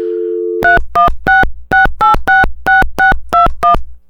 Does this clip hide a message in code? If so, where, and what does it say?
Yes: DTMF from 0.6 to 3.7 s, heard as telephone touch tones, tones of about 140 ms. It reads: 3166866621